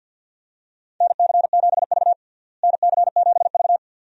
Morse code, heard as a keyed tone, about 52 wpm, 700 Hz. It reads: NC7V NC7V
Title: USA Call Signs - 4 Characters - Repeated 2x - 50wpm